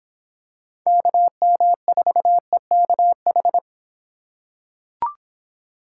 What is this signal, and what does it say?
Morse code, heard as a keyed tone, about 26 wpm, 700 Hz. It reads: KM4EKH